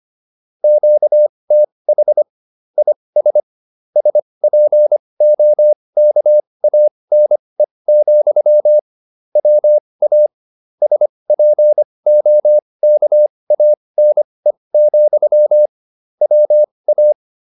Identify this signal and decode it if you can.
Morse code, heard as a keyed tone, about 25 wpm, spaced as for 20 wpm, 600 Hz. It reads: QTH IS SPOKANE, WA SPOKANE, WA
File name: QSO310a 20